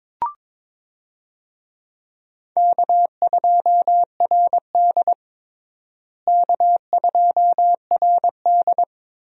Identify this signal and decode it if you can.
Morse code, heard as a keyed tone, about 22 wpm, 700 Hz. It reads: K2RD K2RD
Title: USA Call Signs - 4 Characters - Repeated 2x - 22wpm